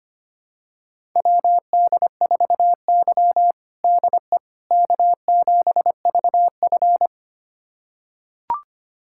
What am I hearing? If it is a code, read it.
Morse code, heard as a keyed tone, about 25 wpm, 700 Hz. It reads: WD4Y DE K7VF
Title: USA Callsign DE Callsign - 4 Characters - 25wpm